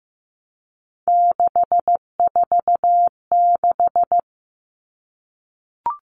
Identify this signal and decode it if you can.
Morse code, heard as a keyed tone, about 15 wpm, 700 Hz. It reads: 646
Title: All Cut Numbers - 3-Digit 15wpm